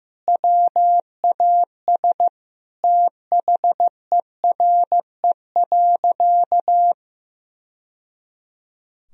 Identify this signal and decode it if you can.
Morse code, heard as a keyed tone, about 16 wpm, 700 Hz. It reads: WAS THERE.
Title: Sentences from Top 1000 Words - Review Part 1 - 15wpm